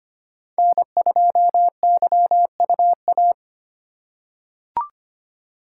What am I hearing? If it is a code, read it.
Morse code, heard as a keyed tone, about 25 wpm, 700 Hz. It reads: N2YUA
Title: USA Call Signs - 5 Characters - 4x Character Spacing - 25wpm